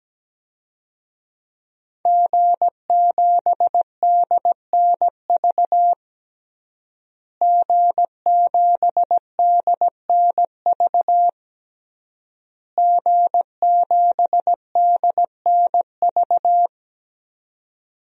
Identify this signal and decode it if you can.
Morse code, heard as a keyed tone, about 17 wpm, 700 Hz. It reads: G7DNV G7DNV G7DNV
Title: DXCC-100 Call Signs - 5 Characters - Repeated 3x - 17wpm